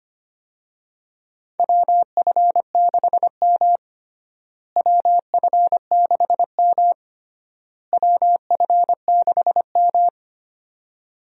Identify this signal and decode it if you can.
Morse code, heard as a keyed tone, about 25 wpm, 700 Hz. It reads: WF6M WF6M WF6M
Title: USA Call Signs - 4 Characters - Repeated 3x - 25wpm